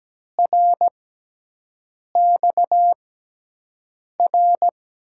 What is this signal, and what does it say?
Morse code, heard as a keyed tone, about 17 wpm, spaced as for 6 wpm, 700 Hz. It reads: RXR
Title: USA Call Signs - 5 Characters - 6x Character Spacing - 17wpm